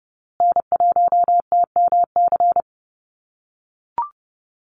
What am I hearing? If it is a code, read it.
Morse code, heard as a keyed tone, about 28 wpm, 700 Hz. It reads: N1TMC
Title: USA Call Signs Repeated 2x 30wpm